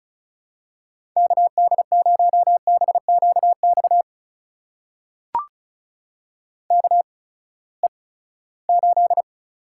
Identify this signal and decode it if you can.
Morse code, heard as a keyed tone, about 36 wpm, spaced as for 20 wpm, 700 Hz. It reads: KD0BQX KE8
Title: USA Call Signs - 6 Characters - 8x Character Spacing - 35wpm